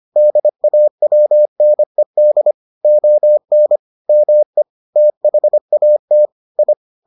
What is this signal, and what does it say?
Morse code, heard as a keyed tone, about 25 wpm, 600 Hz. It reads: DAWNED ON ME THAT I